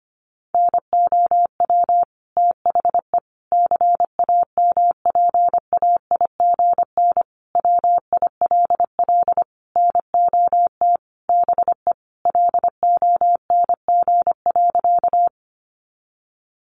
Morse code, heard as a keyed tone, about 25 wpm, 700 Hz. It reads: NOW THE CAMPAIGN WILL NOT BE LONG.
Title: Sentences from Top 1200 Words 25wpm